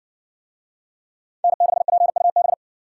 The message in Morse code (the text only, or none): N6PUB